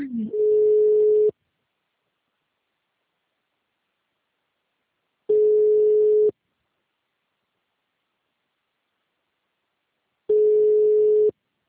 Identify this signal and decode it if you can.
Ringback tone (Europe, 425 Hz), with speech.
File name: out- -2021- - - .4873